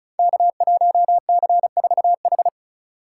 Morse code, heard as a keyed tone, about 36 wpm, 700 Hz. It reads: K1C4H
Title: Code Groups - 5 Letter:Numbers - 35wpm